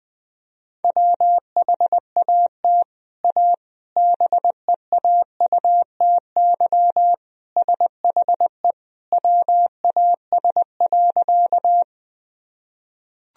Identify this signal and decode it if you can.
Morse code, heard as a keyed tone, about 20 wpm, 700 Hz. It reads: WHAT A BEAUTY SHE WAS.
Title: Sentences from Top 1000 Words - Review Part 2 - 20wpm